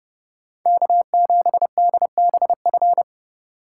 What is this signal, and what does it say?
Morse code, heard as a keyed tone, about 28 wpm, 700 Hz. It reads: K7DBF